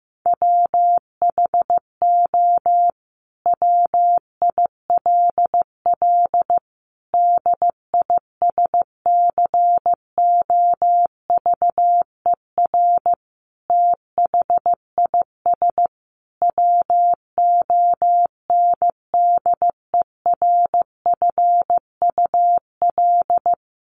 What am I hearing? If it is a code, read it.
Morse code, heard as a keyed tone, about 15 wpm, 700 Hz. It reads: WHO WILL DISCOVER THIS WONDERFUL
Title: Sentences from Top 1100 Words 15wpm